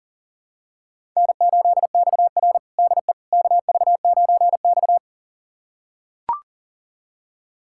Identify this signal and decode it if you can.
Morse code, heard as a keyed tone, about 40 wpm, 700 Hz. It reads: N8XR DE KV9X
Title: USA Callsign DE Callsign - 4 Characters - 40wpm